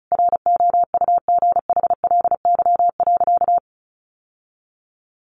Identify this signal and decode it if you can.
Morse code, heard as a keyed tone, about 34 wpm, 700 Hz. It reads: ROUGHLY.
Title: Sentences from Top 1500 Words 35wpm